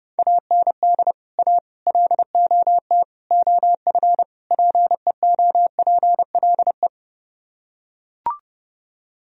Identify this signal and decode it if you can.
Morse code, heard as a keyed tone, about 29 wpm, 700 Hz. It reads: AND A LOT OF PEOPLE